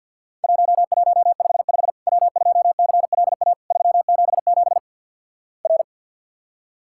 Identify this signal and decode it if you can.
Morse code, heard as a keyed tone, about 50 wpm, 700 Hz. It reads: 1155 W2CLA 377 R
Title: Worked All Europe DX Contest - QTC Exchange - 50wpm